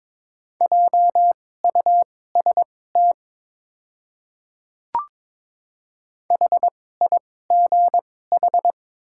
Morse code, heard as a keyed tone, about 22 wpm, spaced as for 16 wpm, 700 Hz. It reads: JUST HIGH
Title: 2x Character Spacing - Top 300 Words - 22wpm